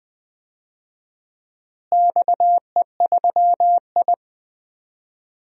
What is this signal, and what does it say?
Morse code, heard as a keyed tone, about 20 wpm, 700 Hz. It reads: XE3I
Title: DXCC-100 Call Signs - 4 Characters - Repeated 2x - 20wpm